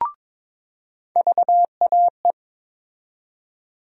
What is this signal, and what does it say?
Morse code, heard as a keyed tone, about 21 wpm, 700 Hz. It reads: VAE
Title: USA Call Sign Postfixes - 3 Letters 22wpm